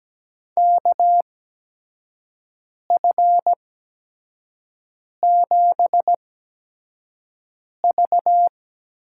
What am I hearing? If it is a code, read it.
Morse code, heard as a keyed tone, about 17 wpm, spaced as for 5 wpm, 700 Hz. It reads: KF7V
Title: USA Call Signs - 6 Characters - 8x Character Spacing - 17wpm